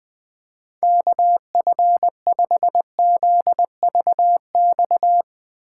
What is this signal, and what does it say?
Morse code, heard as a keyed tone, about 21 wpm, 700 Hz. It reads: KF5ZVX
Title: USA Call Signs - 6 Characters - 2x Character Spacing - 20wpm